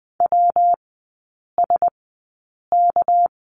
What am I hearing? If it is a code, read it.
Morse code, heard as a keyed tone, about 20 wpm, spaced as for 8 wpm, 690 Hz. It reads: WSK
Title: 198 Lesson 32e - Letter k - Three Characters 20wpm